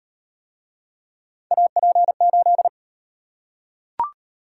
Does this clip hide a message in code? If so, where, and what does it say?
Yes: Morse code from 1.5 to 2.7 s, heard as a keyed tone, about 37 wpm, 700 Hz. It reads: AP8